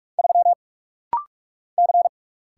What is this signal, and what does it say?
Morse code, heard as a keyed tone, about 44 wpm, 700 Hz. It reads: 3 C
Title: Single Letter-Number - Rapid-Fire 45wpm